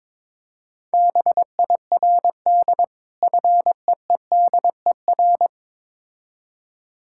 Morse code, heard as a keyed tone, about 22 wpm, 690 Hz. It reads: BIRD FEEDER
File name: Difficult Words with Hints - 22wpm